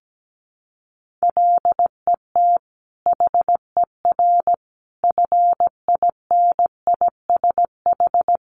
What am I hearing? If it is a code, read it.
Morse code, heard as a keyed tone, about 17 wpm, 700 Hz. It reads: LET HER FINISH
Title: Sentences from Top 1200 Words 17wpm